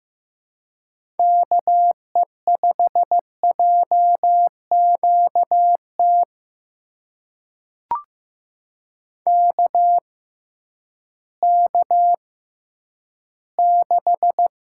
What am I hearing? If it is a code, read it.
Morse code, heard as a keyed tone, about 15 wpm, spaced as for 10 wpm, 700 Hz. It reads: KE5JQT KK6
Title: USA Call Signs - 6 Characters - 6x Character Spacing - 15wpm